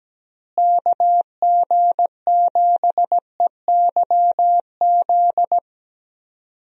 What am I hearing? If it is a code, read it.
Morse code, heard as a keyed tone, about 17 wpm, 700 Hz. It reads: KG7EYZ
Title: USA Call Signs - 6 Characters - Repeated 3x - 17wpm